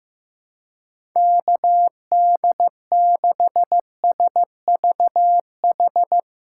Morse code, heard as a keyed tone, about 16 wpm, 700 Hz. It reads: KD6SVH
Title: USA Call Signs - 6 Characters - 3x Character Spacing - 15wpm